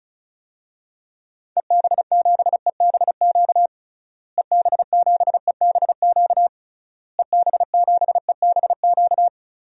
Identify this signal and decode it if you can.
Morse code, heard as a keyed tone, about 36 wpm, 700 Hz. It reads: EB7EBQ EB7EBQ EB7EBQ
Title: DXCC-100 Call Signs - 6 Characters - Repeated 3x - 35wpm